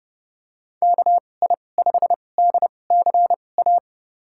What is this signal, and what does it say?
Morse code, heard as a keyed tone, about 32 wpm, spaced as for 22 wpm, 700 Hz. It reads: KI5DCA
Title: USA Call Signs - 2x Character Spacing - 30wpm